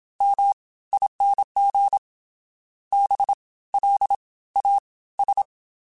Morse code, heard as a keyed tone, about 27 wpm, 800 Hz. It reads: M ING B L A S